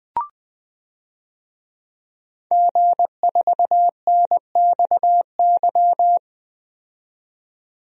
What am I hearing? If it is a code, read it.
Morse code, heard as a keyed tone, about 20 wpm, 700 Hz. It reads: G4NXY
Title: DXCC-100 Call Signs - 5 Characters - 20wpm